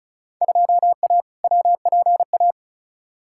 Morse code, heard as a keyed tone, about 36 wpm, 700 Hz. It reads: 2A WPA
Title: Field Day Exchanges - 35wpm